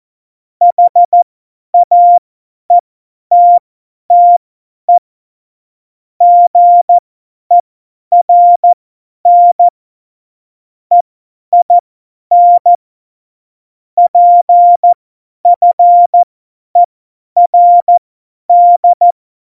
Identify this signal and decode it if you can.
Morse code, heard as a keyed tone, about 14 wpm, spaced as for 10 wpm, 700 Hz. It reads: HAETTE GERN EIN PFERD
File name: Morsekurs / Witze / Witz 19 Mai 2020, 50 F70